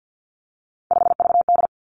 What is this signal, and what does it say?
Morse code, heard as a keyed tone, about 50 wpm, 700 Hz. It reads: 5VD
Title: Three Challenging Letter-Number Combinations 50wpm